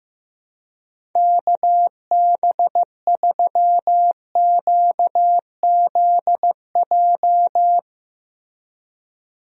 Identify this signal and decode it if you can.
Morse code, heard as a keyed tone, about 15 wpm, 700 Hz. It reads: KB3QZJ
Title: USA Call Signs - 6 Characters - Repeated 3x - 15wpm